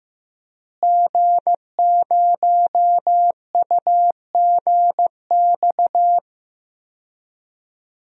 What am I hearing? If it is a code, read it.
Morse code, heard as a keyed tone, about 15 wpm, 700 Hz. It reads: G0UGX
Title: DXCC-100 Call Signs - 5 Characters - Repeated 2x - 15wpm